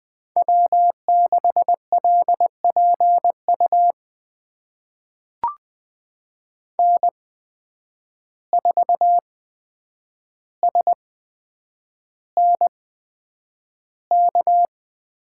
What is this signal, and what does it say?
Morse code, heard as a keyed tone, about 20 wpm, spaced as for 9 wpm, 700 Hz. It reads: W6LPU N4SNK
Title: USA Call Signs - 5 Characters - 8x Character Spacing - 20wpm